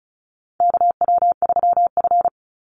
Morse code, heard as a keyed tone, about 34 wpm, 700 Hz. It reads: KW3F
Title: Call Signs - 35wpm